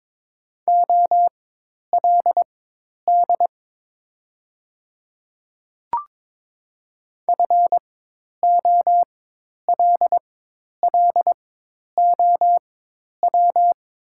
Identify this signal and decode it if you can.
Morse code, heard as a keyed tone, about 22 wpm, spaced as for 10 wpm, 700 Hz. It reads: OLD FOLLOW
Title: 4x Character Spacing - Top 300 Words - 22wpm